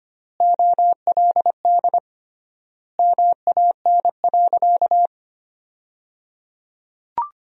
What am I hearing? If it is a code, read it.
Morse code, heard as a keyed tone, about 25 wpm, 700 Hz. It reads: OLD MAN.
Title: 3x Word-Spacing - Sentences from Top 500 Words - 25wpm